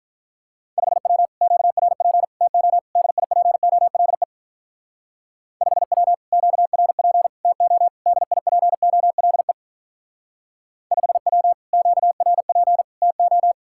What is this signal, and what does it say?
Morse code, heard as a keyed tone, about 42 wpm, 700 Hz. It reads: 5W QRP TO DIPOLE 5W QRP TO DIPOLE 5W QRP TO
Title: Speed-Racing Common QSO Phrases - 35wpm